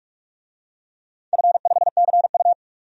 Morse code, heard as a keyed tone, about 46 wpm, 700 Hz. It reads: F5CU
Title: DXCC-100 Call Signs - 4 Characters - 45wpm